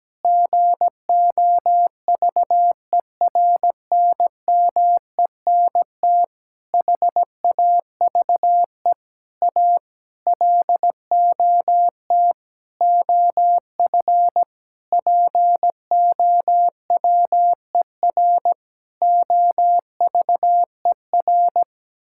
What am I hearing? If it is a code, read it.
Morse code, heard as a keyed tone, about 17 wpm, 700 Hz. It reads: GOVERNMENT HAVE A LOT OF POWER OVER
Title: Sentences from Top 1000 Words - Review Part 5 - 17wpm